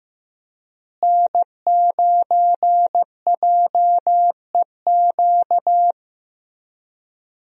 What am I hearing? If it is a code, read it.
Morse code, heard as a keyed tone, about 16 wpm, 700 Hz. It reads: N9JEQ
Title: USA Call Signs - 5 Characters - Repeated 3x - 15wpm